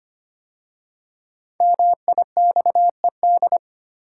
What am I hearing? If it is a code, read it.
Morse code, heard as a keyed tone, about 25 wpm, 700 Hz. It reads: MIXED
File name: 5 Letter Words - First 3 Letters Given - 25wpm